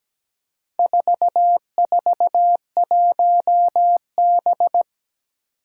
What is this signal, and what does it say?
Morse code, heard as a keyed tone, about 17 wpm, 700 Hz. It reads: 441B
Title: Four Challenging Letter-Number Combinations 17wpm